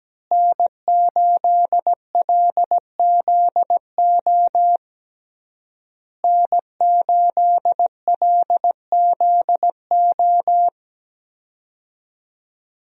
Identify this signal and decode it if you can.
Morse code, heard as a keyed tone, about 17 wpm, 700 Hz. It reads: N8LZO N8LZO
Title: USA Call Signs - 5 Characters - Repeated 3x - 17wpm